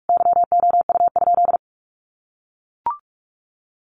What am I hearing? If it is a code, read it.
Morse code, heard as a keyed tone, about 44 wpm, 700 Hz. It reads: YOU?
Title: Sentences from Top 1400 Words 45wpm